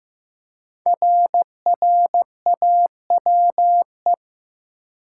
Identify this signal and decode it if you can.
Morse code, heard as a keyed tone, about 15 wpm, 700 Hz. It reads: RRAWE
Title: Code Groups - 5 Letters - 15wpm